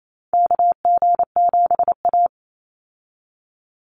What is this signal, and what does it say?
Morse code, heard as a keyed tone, about 28 wpm, 700 Hz. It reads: KG7A
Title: Call Signs - 28wpm